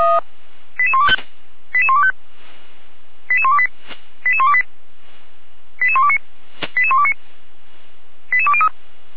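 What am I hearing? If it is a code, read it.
Five-tone selective call (ZVEI), heard as quick runs of five tones, tones of about 70 ms. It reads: 88125 88125 88128 88128 88129 88129 88203
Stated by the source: ZVEI five-tone call received by SDR.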